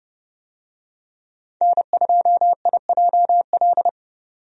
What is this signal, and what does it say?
Morse code, heard as a keyed tone, about 30 wpm, 700 Hz. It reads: N2IJL
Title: USA Call Signs - 5 Characters - 30wpm